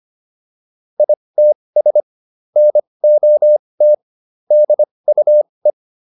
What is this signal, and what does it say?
Morse code, heard as a keyed tone, about 25 wpm, spaced as for 20 wpm, 600 Hz. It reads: ITS NOT DUE